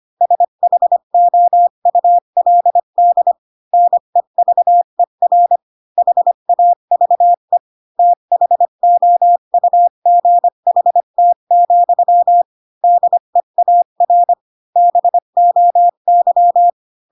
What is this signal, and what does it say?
Morse code, heard as a keyed tone, about 25 wpm, 700 Hz. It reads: SHOULD NEVER HAVE THOUGHT, DEAR BOY
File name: Pinocchio / Pinocchio ch7 20 25 ik1hge